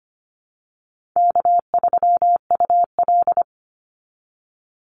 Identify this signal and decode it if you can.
Morse code, heard as a keyed tone, about 25 wpm, 700 Hz. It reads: K3UL
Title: Easy Call Signs Repeated 3x - 25wpm